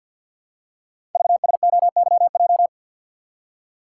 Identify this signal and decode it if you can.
Morse code, heard as a keyed tone, about 50 wpm, 700 Hz. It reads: VSOYJ
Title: Code Groups - 5 Letters - 50wpm